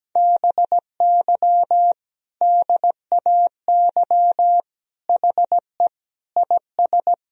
Morse code, heard as a keyed tone, about 17 wpm, 700 Hz. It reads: BY DAY HE IS
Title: Sentences from Top 1000 Words - Review Part 1 - 17wpm